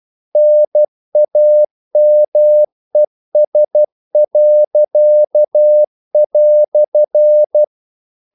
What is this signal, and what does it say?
Morse code, heard as a keyed tone, about 12 wpm, 600 Hz. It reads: NAMES."
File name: CW / Practice / EnglishQuotations0312